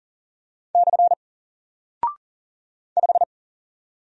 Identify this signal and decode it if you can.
Morse code, heard as a keyed tone, about 40 wpm, 700 Hz. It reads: / 5
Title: Single Character (Letter:Number:Punctuation:Prosign) Alt 40wpm